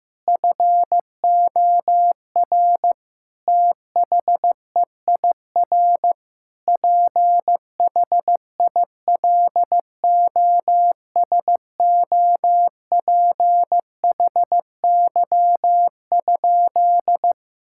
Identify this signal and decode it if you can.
Morse code, heard as a keyed tone, about 15 wpm, 700 Hz. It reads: FOR THEIR PHILOSOPHY?
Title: Sentences from Top 1000 Words - Review Part 6 - 15wpm